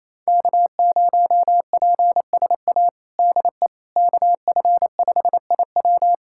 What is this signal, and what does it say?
Morse code, heard as a keyed tone, about 28 wpm, 700 Hz. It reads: K0PSA DE KF5IW
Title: USA Callsign DE Callsign - 5 Characters - 28wpm